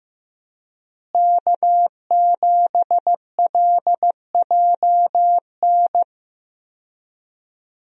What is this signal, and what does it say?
Morse code, heard as a keyed tone, about 15 wpm, 700 Hz. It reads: K7LJN
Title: USA Call Signs - 5 Characters - Repeated 2x - 15wpm